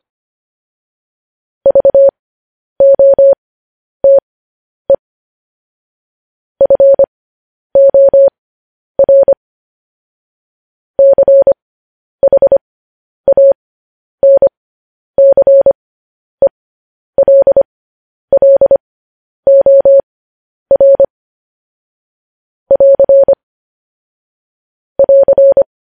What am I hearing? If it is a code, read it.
Morse code, heard as a keyed tone, about 25 wpm, spaced as for 10 wpm, 550 Hz. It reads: VOTE FOR CHANCELLOR <AR> <AR>
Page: news-headlines-in-morse-code-at-10-wpm